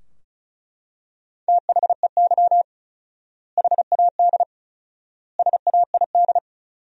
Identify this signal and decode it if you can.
Morse code, heard as a keyed tone, about 34 wpm, 700 Hz. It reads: THEY HAD SAID
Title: 4x Word-Spacing - Sentences from Top 400 Words - 35wpm